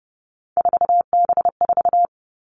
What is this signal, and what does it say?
Morse code, heard as a keyed tone, about 30 wpm, 700 Hz. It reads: 4B4
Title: Three Challenging Letter-Number Combinations 30wpm